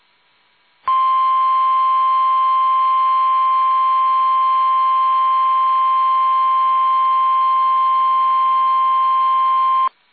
Emergency Alert System (SAME) tones (the 1050 Hz attention tone). It is heard from 0.8 to 9.9 s.